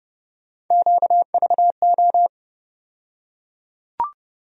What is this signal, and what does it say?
Morse code, heard as a keyed tone, about 30 wpm, 700 Hz. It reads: QVO
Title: USA Call Sign Postfixes - 3 Letters 30wpm